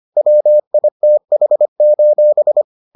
Morse code, heard as a keyed tone, about 25 wpm, 600 Hz. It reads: WITH: